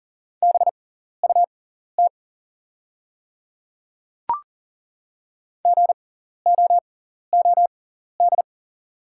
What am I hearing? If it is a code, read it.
Morse code, heard as a keyed tone, about 39 wpm, spaced as for 14 wpm, 700 Hz. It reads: BUT GOOD
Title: 6x Character Spacing - Top 300 Words - 40wpm